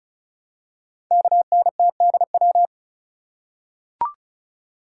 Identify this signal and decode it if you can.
Morse code, heard as a keyed tone, about 36 wpm, 690 Hz. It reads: KNTDW
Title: Code Groups - 5 Letters - 35wpm